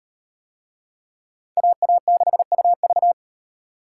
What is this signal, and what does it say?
Morse code, heard as a keyed tone, about 36 wpm, 700 Hz. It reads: AA6UV